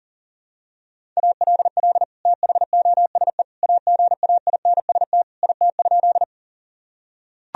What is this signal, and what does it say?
Morse code, heard as a keyed tone, about 37 wpm, 700 Hz. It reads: ALL THOSE AGAINST IT?
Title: Sentences from Top 1000 Words - Review Part 1 - 40wpm